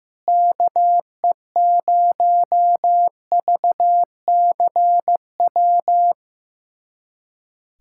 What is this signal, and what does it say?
Morse code, heard as a keyed tone, about 15 wpm, 700 Hz. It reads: KE0VCW
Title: USA Callsign DE Callsign - 6 Characters - 15wpm